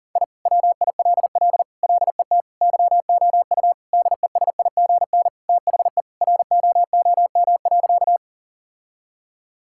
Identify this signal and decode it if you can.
Morse code, heard as a keyed tone, about 37 wpm, 700 Hz. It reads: I WILL LET YOU DESIGN THE ROOM.
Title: Sentences from Top 1000 Words - Review Part 5 - 40wpm